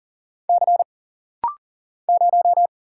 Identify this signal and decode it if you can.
Morse code, heard as a keyed tone, about 40 wpm, 700 Hz. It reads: C 0
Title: Single Letter-Number Alt - Rapid-Fire 40wpm